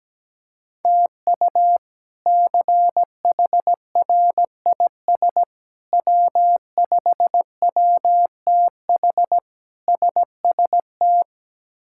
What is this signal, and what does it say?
Morse code, heard as a keyed tone, about 17 wpm, 700 Hz. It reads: TU CHRIS W5WTH SST
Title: K1USN SST QSO Practice - 17wpm